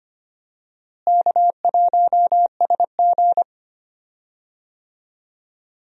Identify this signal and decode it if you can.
Morse code, heard as a keyed tone, about 25 wpm, 700 Hz. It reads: K1SG
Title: USA Call Signs - 4 Characters - Repeated 2x - 25wpm